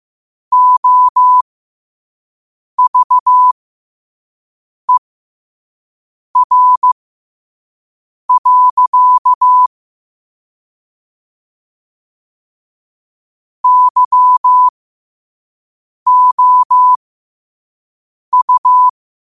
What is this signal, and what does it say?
Morse code, heard as a keyed tone, about 15 wpm, spaced as for 5 wpm, 1000 Hz. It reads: OVER. YOU
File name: qotdfiles / 5wpm